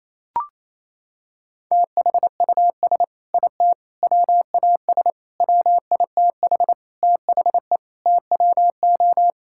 Morse code, heard as a keyed tone, about 28 wpm, 700 Hz. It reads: THUS IT WAS WITH THE TWO
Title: Sentences from Top 1000 Words - Review Part 5 - 28wpm